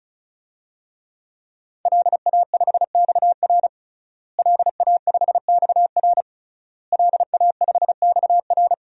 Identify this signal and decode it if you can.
Morse code, heard as a keyed tone, about 34 wpm, 700 Hz. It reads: LA5XR LA5XR LA5XR
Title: DXCC-100 Call Signs - 5 Characters - Repeated 3x - 35wpm